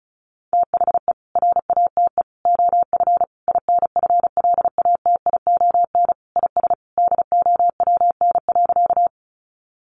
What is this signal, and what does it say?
Morse code, heard as a keyed tone, about 34 wpm, 700 Hz. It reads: THE RATE OF INFLATION IS DOWN.
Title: Sentences from Top 1100 Words 35wpm